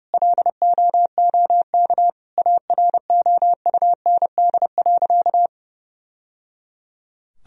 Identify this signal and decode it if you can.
Morse code, heard as a keyed tone, about 30 wpm, 700 Hz. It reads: LOOK AROUND.
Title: Sentences from Top 1000 Words - Review Part 3 - 30wpm